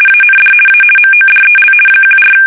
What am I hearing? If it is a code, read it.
RTTY, about 50 baud. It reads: AND A MODEL 14
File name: fsk850-50B / teletype-corp